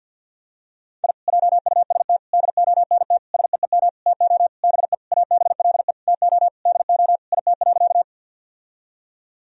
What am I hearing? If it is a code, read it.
Morse code, heard as a keyed tone, about 50 wpm, 700 Hz. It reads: I JUST DONT SEEM TO BE ABLE TO DO IT.
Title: Sentences from Top 1000 Words - Review Part 2 - 50wpm